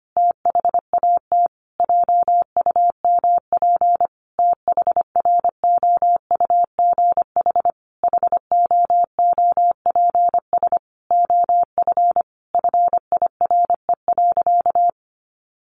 Morse code, heard as a keyed tone, about 25 wpm, 700 Hz. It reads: THAT JUMP THROUGH HOOPS OF FIRE.
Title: Easy Phrases 25wpm